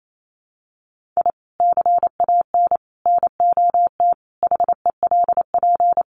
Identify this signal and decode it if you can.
Morse code, heard as a keyed tone, about 28 wpm, 700 Hz. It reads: I CAN NOT HELP
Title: Sets of 5 Words Encore2 28wpm